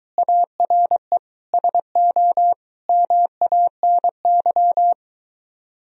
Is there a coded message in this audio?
Yes: Morse code from 0.2 to 4.9 s, heard as a keyed tone, about 23 wpm, 700 Hz. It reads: ARE SO MANY